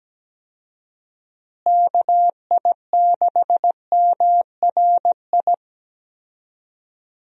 Morse code, heard as a keyed tone, about 17 wpm, 700 Hz. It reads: KI6MRI
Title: USA Call Signs - 6 Characters - Repeated 3x - 17wpm